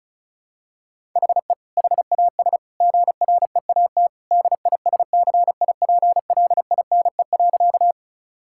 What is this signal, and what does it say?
Morse code, heard as a keyed tone, about 36 wpm, 700 Hz. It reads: HE HAS GREAT DISCIPLINE.
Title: Sentences from Top 1000 Words - Review Part 6 - 35wpm